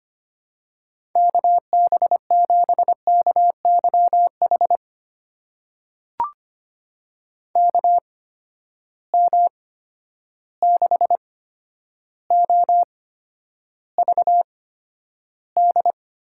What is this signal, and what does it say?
Morse code, heard as a keyed tone, about 25 wpm, spaced as for 11 wpm, 700 Hz. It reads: KB7KYH KM6OVD
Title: USA Call Signs - 6 Characters - 8x Character Spacing - 25wpm